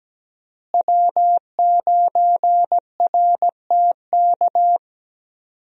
Morse code, heard as a keyed tone, about 17 wpm, 700 Hz. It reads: W9RTK
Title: USA Call Signs - 5 Characters - Repeated 2x - 17wpm